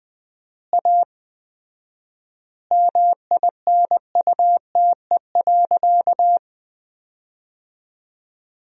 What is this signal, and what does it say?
Morse code, heard as a keyed tone, about 20 wpm, 700 Hz. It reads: A MINUTE.
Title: 4x Word-Spacing - Sentences from Top 500 Words - 20wpm